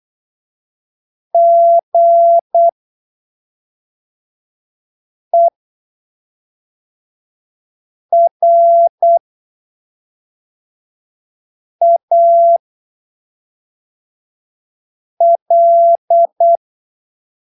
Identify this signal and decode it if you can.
Morse code, heard as a keyed tone, about 8 wpm, spaced as for 3 wpm, 680 Hz. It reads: GERAL
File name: arquivos / texto / texto 01 08ppm